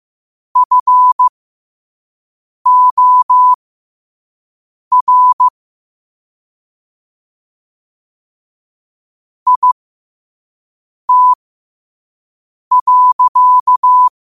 Morse code, heard as a keyed tone, about 15 wpm, spaced as for 5 wpm, 1000 Hz. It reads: FOR IT.
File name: qotdfiles / 5wpm